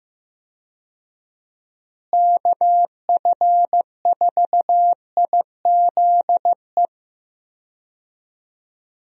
Morse code, heard as a keyed tone, about 15 wpm, 700 Hz. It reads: KF4IZE
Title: USA Call Signs - 6 Characters - Repeated 2x - 15wpm